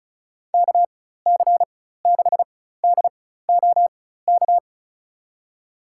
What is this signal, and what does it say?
Morse code, heard as a keyed tone, about 36 wpm, spaced as for 16 wpm, 690 Hz. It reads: KC6DOK
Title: USA Call Signs - 6 Characters - 4x Character Spacing - 35wpm